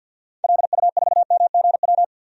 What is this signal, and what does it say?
Morse code, heard as a keyed tone, about 52 wpm, 700 Hz. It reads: LU4MGW
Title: DXCC-100 Call Signs - 6 Characters - 50wpm